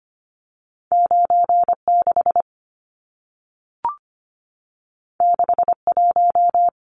Morse code, heard as a keyed tone, about 25 wpm, 700 Hz. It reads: 96 61
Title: Numbers - 2-Digit 25wpm